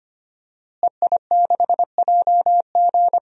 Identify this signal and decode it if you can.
Morse code, heard as a keyed tone, about 25 wpm, 700 Hz. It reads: EI6JG
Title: DXCC-100 Call Signs - 5 Characters - Repeated 3x - 25wpm